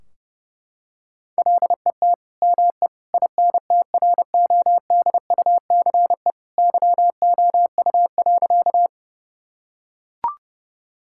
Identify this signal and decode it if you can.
Morse code, heard as a keyed tone, about 30 wpm, 700 Hz. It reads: LET ME INTRODUCE YOU.
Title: Sentences from Top 1600 Words 30wpm